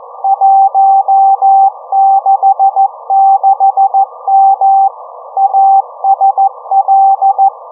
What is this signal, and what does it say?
Morse code, heard as a keyed tone, about 14 wpm, 780 Hz. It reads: 166M ASL